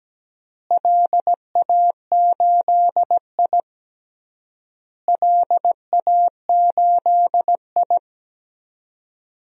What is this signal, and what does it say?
Morse code, heard as a keyed tone, about 17 wpm, 700 Hz. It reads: LA8I LA8I
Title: DXCC-100 Call Signs - 4 Characters - Repeated 3x - 17wpm